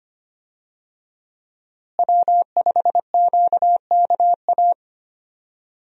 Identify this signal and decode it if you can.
Morse code, heard as a keyed tone, about 25 wpm, 700 Hz. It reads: W5QKA
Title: USA Call Signs - 5 Characters - Repeated 2x - 25wpm